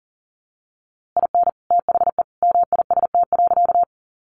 Morse code, heard as a keyed tone, about 40 wpm, 700 Hz. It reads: IN THE MIST.
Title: A Sound Of Thunder 40wpm